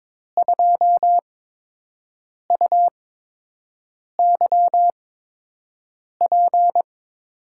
Morse code, heard as a keyed tone, about 22 wpm, spaced as for 6 wpm, 700 Hz. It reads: 2UYP
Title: USA Call Signs - 5 Characters - 8x Character Spacing - 22wpm